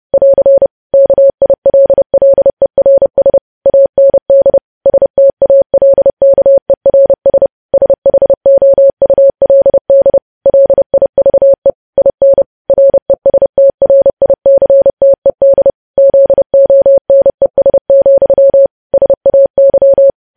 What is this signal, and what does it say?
Morse code, heard as a keyed tone, about 30 wpm, 550 Hz. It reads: <AR> KILLERS AND STALKERS SHOULD LIVE IN RESTRICTED ZONES, SAY
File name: podcast / News headlines-1x-30x30